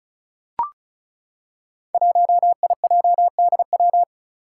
Morse code, heard as a keyed tone, about 34 wpm, 700 Hz. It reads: 1IJDW
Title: Five Challenging Letter-Number Combinations 35wpm